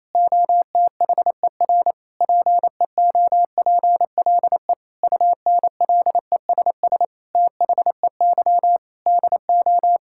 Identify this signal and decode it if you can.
Morse code, heard as a keyed tone, about 28 wpm, 700 Hz. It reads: OTHER PEOPLE UNLESS THEY DO
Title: Sentences from Top 1000 Words - Review Part 6 - 28wpm